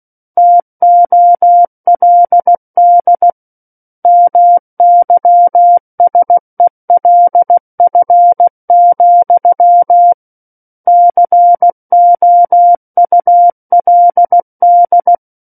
Morse code, heard as a keyed tone, about 16 wpm, 700 Hz. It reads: TOLD MYSELF, COULD
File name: Book One / 07-Chapter 7 - 16 WPM